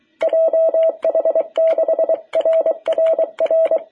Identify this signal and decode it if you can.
Morse code, heard as a keyed tone, about 23 wpm, 660 Hz. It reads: JH6RRR
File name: jh8cbh / 2021cw